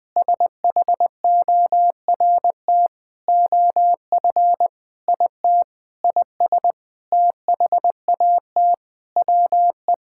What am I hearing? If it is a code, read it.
Morse code, heard as a keyed tone, about 20 wpm, 700 Hz. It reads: SHORT OF IT IS THAT WE
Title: Sentences with 9-Word Phrases - 20wpm